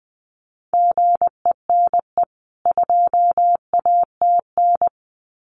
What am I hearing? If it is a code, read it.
Morse code, heard as a keyed tone, about 20 wpm, 700 Hz. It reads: GENE 2ATN
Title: CWT Exchange 20wpm